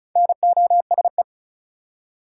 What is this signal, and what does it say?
Morse code, heard as a keyed tone, about 36 wpm, 700 Hz. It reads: NOSE
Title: Four Letter Words - 35wpm